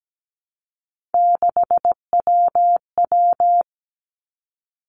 Morse code, heard as a keyed tone, about 17 wpm, 700 Hz. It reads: 6WW
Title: Three Challenging Letter-Number Combinations 17wpm